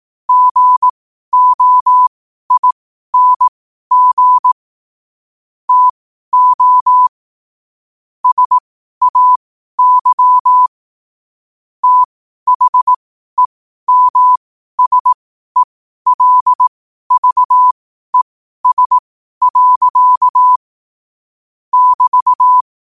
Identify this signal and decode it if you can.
Morse code, heard as a keyed tone, about 18 wpm, spaced as for 12 wpm, 1000 Hz. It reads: GOING TO SAY THEMSELVES. <BT>
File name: qotdfiles / 12wpm